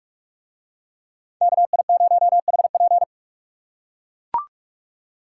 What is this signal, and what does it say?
Morse code, heard as a keyed tone, about 44 wpm, 700 Hz. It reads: KI0HP